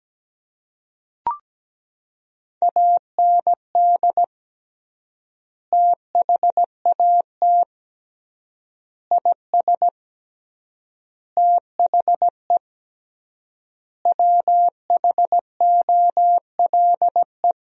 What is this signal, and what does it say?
Morse code, heard as a keyed tone, about 17 wpm, 700 Hz. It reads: AND THAT IS THE WHOLE
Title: 3x Word-Spacing - Sentences from Top 400 Words - 17wpm